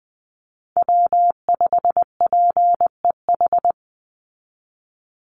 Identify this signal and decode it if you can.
Morse code, heard as a keyed tone, about 20 wpm, 700 Hz. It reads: W5PEH
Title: Call Signs Repeated 2x - 20wpm